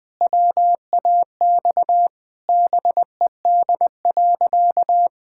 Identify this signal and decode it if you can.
Morse code, heard as a keyed tone, about 20 wpm, 700 Hz. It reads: WAX BED.
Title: Pangram Challenge v2 - 20wpm